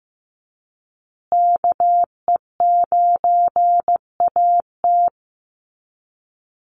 Morse code, heard as a keyed tone, about 15 wpm, 700 Hz. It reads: KE9AT
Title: USA Call Signs Repeated 2x 15wpm